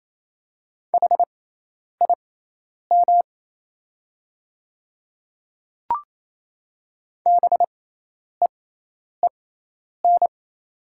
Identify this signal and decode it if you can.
Morse code, heard as a keyed tone, about 28 wpm, spaced as for 10 wpm, 690 Hz. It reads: HIM BEEN
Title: 6x Character Spacing - Top 200 Words - 28wpm